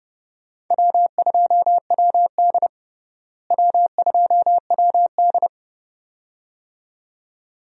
Morse code, heard as a keyed tone, about 30 wpm, 700 Hz. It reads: W2WD W2WD
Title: USA Call Signs - 4 Characters - Repeated 2x - 30wpm